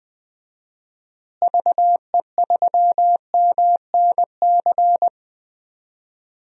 Morse code, heard as a keyed tone, about 20 wpm, 700 Hz. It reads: VE3MNC
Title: DXCC-100 Call Signs - 6 Characters - Repeated 3x - 20wpm